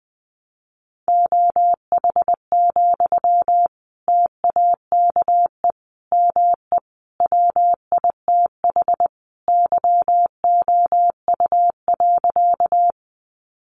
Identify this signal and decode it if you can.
Morse code, heard as a keyed tone, about 20 wpm, 700 Hz. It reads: OH, TAKE ME WITH YOU.
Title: Sentences from Top 500 Words 20wpm